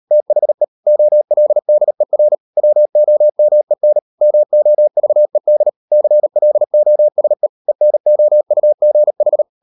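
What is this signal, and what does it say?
Morse code, heard as a keyed tone, about 38 wpm, 600 Hz. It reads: THE OLDER WOMEN MOVED CLOSE ENOUGH
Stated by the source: Morse code practice at 38 WPM, sending Ken Follett's novel Never in English, from CW med Gnister.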